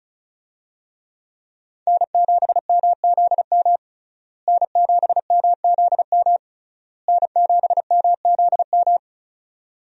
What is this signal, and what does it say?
Morse code, heard as a keyed tone, about 36 wpm, 700 Hz. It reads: N7MZM N7MZM N7MZM
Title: USA Call Signs - 5 Characters - Repeated 3x - 35wpm